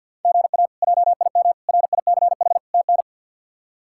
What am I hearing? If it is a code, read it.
Morse code, heard as a keyed tone, about 48 wpm, 700 Hz. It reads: GA JIM RICH TN
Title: K1USN SST QSO Practice - 50wpm